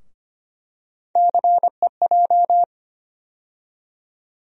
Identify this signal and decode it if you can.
Morse code, heard as a keyed tone, about 25 wpm, 700 Hz. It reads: CEJ
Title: USA Call Sign Postfixes - 3 Letters 25wpm